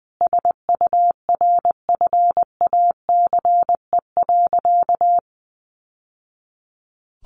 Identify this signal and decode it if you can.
Morse code, heard as a keyed tone, about 20 wpm, 700 Hz. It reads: SURFACE.
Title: Sentences from Top 600 Words 20wpm